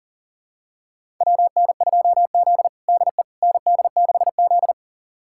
Morse code, heard as a keyed tone, about 40 wpm, 700 Hz. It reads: WN2Z DE ND6Z
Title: USA Callsign DE Callsign - 4 Characters - 40wpm